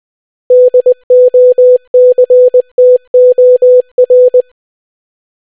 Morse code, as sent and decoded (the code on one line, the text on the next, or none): -.. --- -.-. - --- .-.
DOCTOR